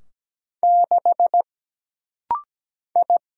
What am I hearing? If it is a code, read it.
Morse code, heard as a keyed tone, about 17 wpm, 700 Hz. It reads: 6 I
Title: Single Letter-Number - Rapid-Fire 17wpm